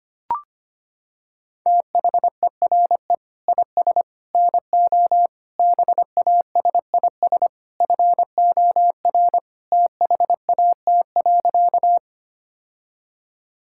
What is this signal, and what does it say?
Morse code, heard as a keyed tone, about 25 wpm, 700 Hz. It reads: THERE IS NO BASIS FOR THAT.
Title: Sentences from Top 1000 Words - Review Part 5 - 25wpm